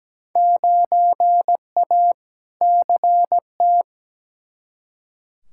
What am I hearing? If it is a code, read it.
Morse code, heard as a keyed tone, about 17 wpm, 700 Hz. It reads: 9A CT
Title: Field Day Exchanges Encore - 17wpm